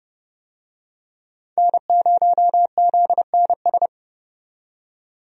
Morse code, heard as a keyed tone, about 30 wpm, 700 Hz. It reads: N0ZNS
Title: USA Call Signs - 5 Characters - 30wpm